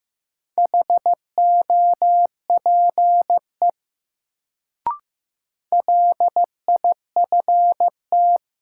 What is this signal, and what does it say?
Morse code, heard as a keyed tone, about 15 wpm, 700 Hz. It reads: HOPE LIFT
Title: Sentences from Top 1000 Words - New Word Drill - 15wpm